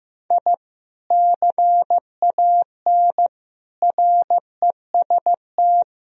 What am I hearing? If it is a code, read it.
Morse code, heard as a keyed tone, about 16 wpm, 700 Hz. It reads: I CAN REST
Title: Sentences from Top 1000 Words - Review Part 1 - 15wpm